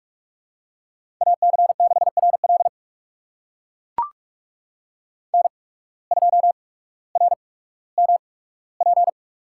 Morse code, heard as a keyed tone, about 44 wpm, spaced as for 19 wpm, 700 Hz. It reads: AC6RL N2RMP
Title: USA Call Signs - 5 Characters - 8x Character Spacing - 45wpm